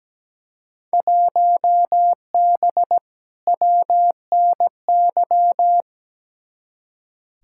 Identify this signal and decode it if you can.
Morse code, heard as a keyed tone, about 17 wpm, 700 Hz. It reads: 1B WNY
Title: Field Day Exchanges - 17wpm